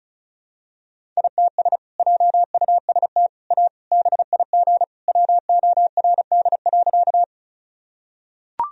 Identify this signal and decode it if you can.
Morse code, heard as a keyed tone, about 36 wpm, 700 Hz. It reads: ITS JUST A BIG WORD.
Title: Sentences from Top 1000 Words - Review Part 1 - 35wpm